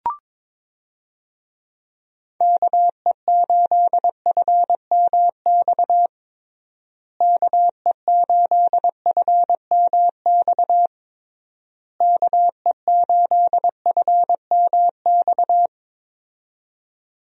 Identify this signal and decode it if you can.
Morse code, heard as a keyed tone, about 22 wpm, 700 Hz. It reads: KE8FMX KE8FMX KE8FMX
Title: USA Call Signs - 6 Characters - Repeated 3x - 22wpm